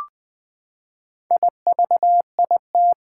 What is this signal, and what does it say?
Morse code, heard as a keyed tone, about 20 wpm, 700 Hz. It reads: IVIT
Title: Four Challenging Letter-Number Combinations 20wpm